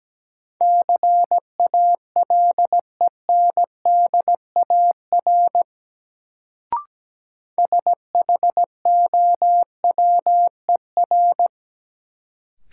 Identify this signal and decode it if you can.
Morse code, heard as a keyed tone, about 17 wpm, 700 Hz. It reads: CALENDAR SHOWER
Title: Sentences from Top 1600 Words - New Word Drill Alt - 17wpm